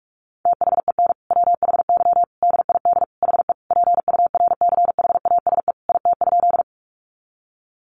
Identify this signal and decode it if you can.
Morse code, heard as a keyed tone, about 44 wpm, 700 Hz. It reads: THEN WHY DID HE PURCHASE IT?
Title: Sentences from Top 800 Words 45wpm